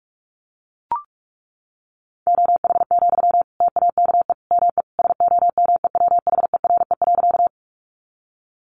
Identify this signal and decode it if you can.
Morse code, heard as a keyed tone, about 44 wpm, 700 Hz. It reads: OH, TAKE ME SOMEWHERE.
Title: Sentences from Top 1400 Words 45wpm